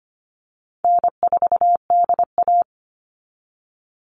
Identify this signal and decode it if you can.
Morse code, heard as a keyed tone, about 25 wpm, 700 Hz. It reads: N4DA
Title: NA SOTA Chaser Callsigns 25wpm